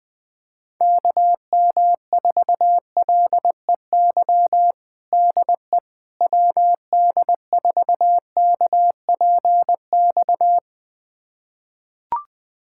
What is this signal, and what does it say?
Morse code, heard as a keyed tone, about 20 wpm, 700 Hz. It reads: KM4LEY DE WD4KPX
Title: USA Callsign DE Callsign 20wpm